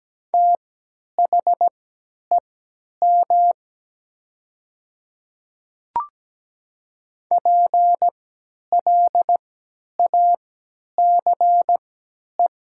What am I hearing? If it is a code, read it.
Morse code, heard as a keyed tone, about 17 wpm, spaced as for 10 wpm, 700 Hz. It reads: THEM PLACE
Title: 3x Character Spacing - Top 300 Words - 17wpm